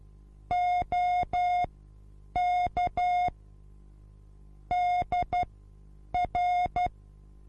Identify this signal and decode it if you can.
Morse code, heard as a keyed tone, about 12 wpm, spaced as for 8 wpm, 720 Hz. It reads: OK DR
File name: repos / Morsekurs / DJ4UF Morsen24